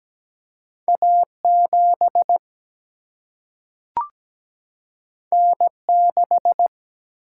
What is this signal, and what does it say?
Morse code, heard as a keyed tone, about 17 wpm, 700 Hz. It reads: A7 N6
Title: Single-Letter Single-Number 17wpm